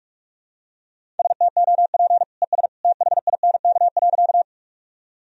Morse code, heard as a keyed tone, about 46 wpm, 700 Hz. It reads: STOP ES THINK.
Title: Sentences from Top 300 Words and Abbreviations 45wpm